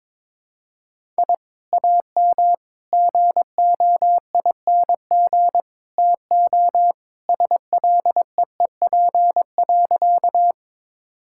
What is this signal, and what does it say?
Morse code, heard as a keyed tone, about 22 wpm, 700 Hz. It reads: I AM GOING TO SLEEP.
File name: practice / 22wpm / Sentences with 2-Word Phrases - 22wpm